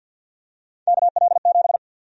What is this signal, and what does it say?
Morse code, heard as a keyed tone, about 52 wpm, 700 Hz. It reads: KL7
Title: USA Call Sign Prefixes - Repeated 2x - 50wpm